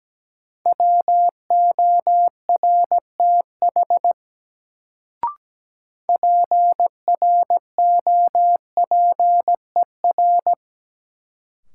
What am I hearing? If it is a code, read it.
Morse code, heard as a keyed tone, about 17 wpm, 700 Hz. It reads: WORTH PROPER
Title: Sentences from Top 800 Words - New Word Drill - 17wpm